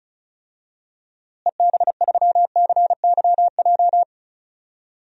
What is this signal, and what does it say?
Morse code, heard as a keyed tone, about 36 wpm, 700 Hz. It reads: EB3CYJ
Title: DXCC-100 Call Signs - 6 Characters - Repeated 2x - 35wpm